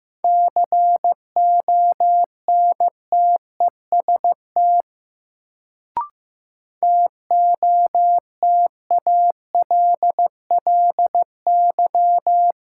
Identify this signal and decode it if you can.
Morse code, heard as a keyed tone, about 15 wpm, 700 Hz. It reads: CONTEST TOTALLY
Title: Sentences from Top 1600 Words - New Word Drill Alt - 15wpm